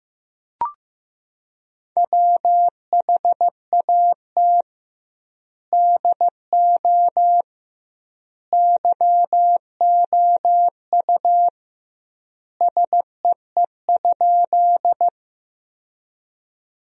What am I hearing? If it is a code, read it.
Morse code, heard as a keyed tone, about 16 wpm, 700 Hz. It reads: WHAT DO YOU SEE?
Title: 2x Word-Spacing - Sentences from Top 100 Words - 15wpm